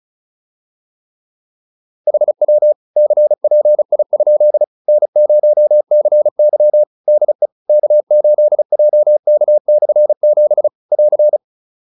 Morse code, heard as a keyed tone, about 34 wpm, 600 Hz. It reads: HW CPI? N0CY DE K8JK/7 <AR>
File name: QSO310 35